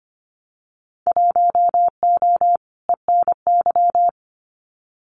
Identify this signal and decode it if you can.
Morse code, heard as a keyed tone, about 25 wpm, 700 Hz. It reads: 1O ENY
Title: Winter Field Day Exchanges 25wpm